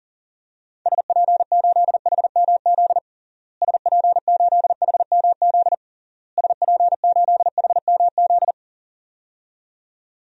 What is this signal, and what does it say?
Morse code, heard as a keyed tone, about 40 wpm, 700 Hz. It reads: SP8HMZ SP8HMZ SP8HMZ
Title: DXCC-100 Call Signs - 6 Characters - Repeated 3x - 40wpm